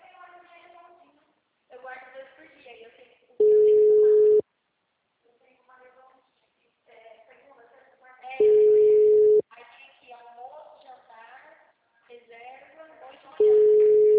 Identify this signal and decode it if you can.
Ringback tone (Europe, 425 Hz), with speech.